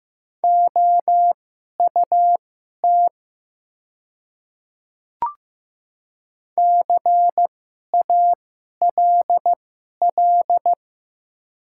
Morse code, heard as a keyed tone, about 15 wpm, spaced as for 11 wpm, 700 Hz. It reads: OUT CALL
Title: 2x Character Spacing - Top 100 Words - 15wpm